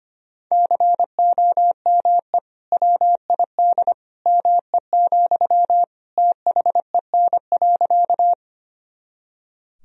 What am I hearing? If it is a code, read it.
Morse code, heard as a keyed tone, about 25 wpm, 700 Hz. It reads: COME WID ME, THEN.
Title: Sentences from Top 200 Words and Abbreviations 25wpm